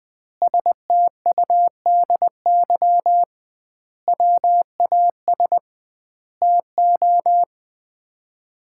Morse code, heard as a keyed tone, about 20 wpm, 700 Hz. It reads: STUDY WAS TO
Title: 2x Word-Spacing - Sets of 5 Words - 20wpm